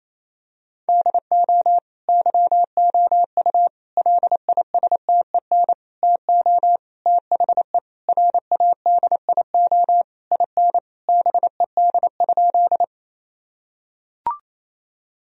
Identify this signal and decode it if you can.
Morse code, heard as a keyed tone, about 28 wpm, 700 Hz. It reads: DO YOU LISTEN TO THE RADIO IN BED?
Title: Sentences from Top 1000 Words - Review Part 3 - 28wpm